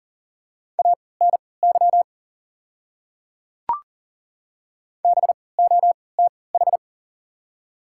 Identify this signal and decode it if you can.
Morse code, heard as a keyed tone, about 40 wpm, spaced as for 23 wpm, 700 Hz. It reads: ANY BOTH
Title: 3x Character Spacing - Top 300 Words - 40wpm